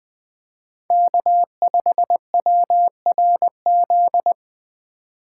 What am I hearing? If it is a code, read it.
Morse code, heard as a keyed tone, about 20 wpm, 700 Hz. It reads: K5WRZ
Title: USA Call Signs - 5 Characters - 4x Character Spacing - 20wpm